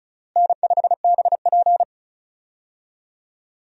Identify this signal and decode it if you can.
Morse code, heard as a keyed tone, about 36 wpm, 700 Hz. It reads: N5BP
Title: USA Call Signs - 4 Characters - Repeated 3x - 35wpm